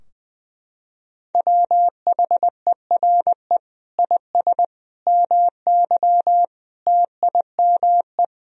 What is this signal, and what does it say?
Morse code, heard as a keyed tone, about 21 wpm, 700 Hz. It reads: WHERE IS MY TIME
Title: Sentences from Top 1000 Words - Review Part 2 - 20wpm